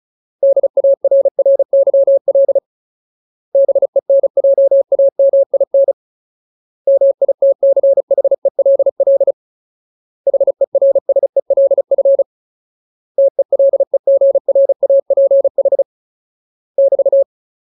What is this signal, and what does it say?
Morse code, heard as a keyed tone, about 37 wpm, 550 Hz. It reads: DARRYL BENJAMIN MITCHELL HERSELF TELEGRAPH <BT>
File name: Ultimate Word List / 35WPM Ultimate